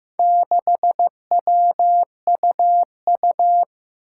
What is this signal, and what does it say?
Morse code, heard as a keyed tone, about 15 wpm, 700 Hz. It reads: 6WUU
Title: Four Challenging Letter-Number Combinations 15wpm